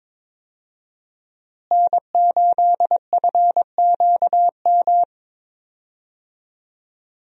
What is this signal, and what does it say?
Morse code, heard as a keyed tone, about 22 wpm, 700 Hz. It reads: N8FQM
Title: USA Call Signs - 5 Characters - Repeated 2x - 22wpm